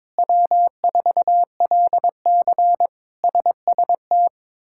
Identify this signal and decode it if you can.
Morse code, heard as a keyed tone, about 21 wpm, 700 Hz. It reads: W4LC SST
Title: K1USN SST QSO Practice - 22wpm